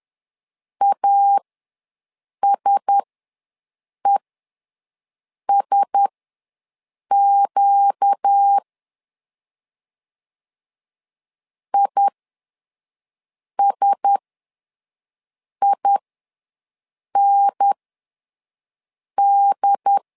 Morse code, heard as a keyed tone, about 10 wpm, spaced as for 5 wpm, 800 Hz. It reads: ASESQ ISIND
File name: LEZIONI / 4 Q1S ws 10wpm 2208